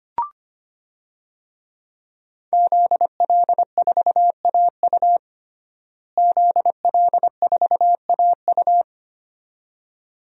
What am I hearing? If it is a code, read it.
Morse code, heard as a keyed tone, about 25 wpm, 700 Hz. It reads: ZL4AU ZL4AU
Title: DXCC-100 Call Signs - 5 Characters - Repeated 2x - 25wpm